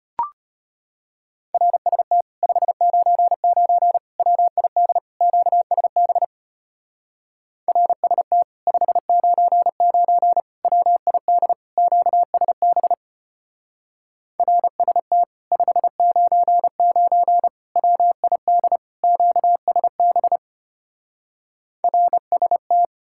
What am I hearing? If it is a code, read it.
Morse code, heard as a keyed tone, about 33 wpm, 700 Hz. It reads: RST 599 WID QSB RST 599 WID QSB RST 599 WID QSB RST